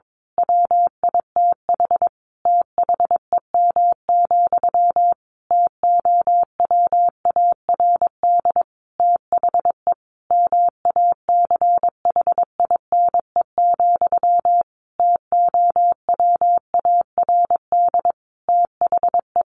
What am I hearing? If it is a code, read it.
Morse code, heard as a keyed tone, about 22 wpm, 700 Hz. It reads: WITH THEM, TOWARD THE MACHINE, TOWARD THE
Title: A Sound Of Thunder 22wpm